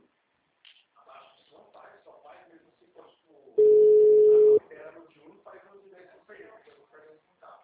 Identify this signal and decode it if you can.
Ringback tone (Europe, 425 Hz), with speech.